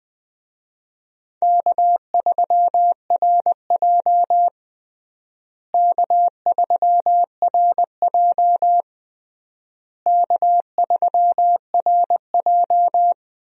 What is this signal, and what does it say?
Morse code, heard as a keyed tone, about 20 wpm, 700 Hz. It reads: K3RJ K3RJ K3RJ
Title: USA Call Signs - 4 Characters - Repeated 3x - 20wpm